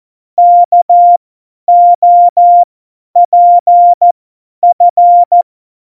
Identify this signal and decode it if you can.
Morse code, heard as a keyed tone, about 14 wpm, spaced as for 10 wpm, 700 Hz. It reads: KOPF
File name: Morsekurs / Witze / Witz 15. Oktober 2019, 50 F70